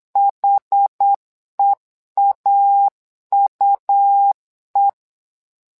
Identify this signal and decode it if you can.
Morse code, heard as a keyed tone, about 8 wpm, 800 Hz. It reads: HEAUE
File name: LEZIONI / 2 UHDN / 2 UH 8wpm 2129